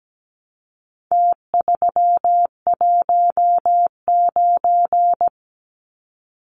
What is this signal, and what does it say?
Morse code, heard as a keyed tone, about 17 wpm, 700 Hz. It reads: T319
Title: Numbers - Contest Style - 4-Digit 17wpm